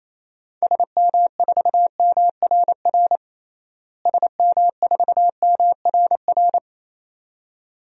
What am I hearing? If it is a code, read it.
Morse code, heard as a keyed tone, about 28 wpm, 700 Hz. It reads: SM4MRR SM4MRR
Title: DXCC-100 Call Signs - 6 Characters - Repeated 2x - 28wpm